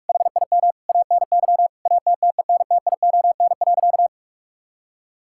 Morse code, heard as a keyed tone, about 46 wpm, 700 Hz. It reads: HIM ANY ATTENTION.
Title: Sentences from Top 1000 Words - Review Part 6 - 45wpm